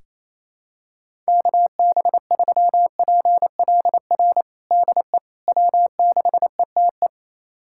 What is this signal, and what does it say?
Morse code, heard as a keyed tone, about 28 wpm, 700 Hz. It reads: KB3PLR DE W6ETE
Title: USA Callsign DE Callsign 28wpm